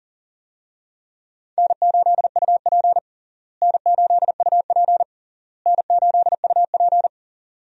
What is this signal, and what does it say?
Morse code, heard as a keyed tone, about 40 wpm, 700 Hz. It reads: N8UP N8UP N8UP
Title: USA Call Signs - 4 Characters - Repeated 3x - 40wpm